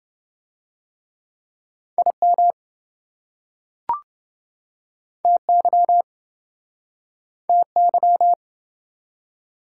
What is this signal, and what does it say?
Morse code, heard as a keyed tone, about 28 wpm, 700 Hz. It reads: IM TY TY